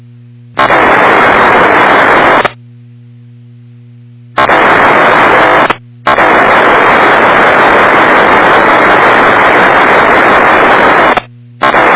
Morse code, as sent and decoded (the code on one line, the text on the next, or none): none